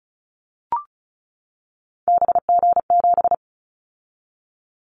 Morse code, heard as a keyed tone, about 34 wpm, 700 Hz. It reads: BG7
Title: Callsign Prefixes for Top 100 DXCC - 35wpm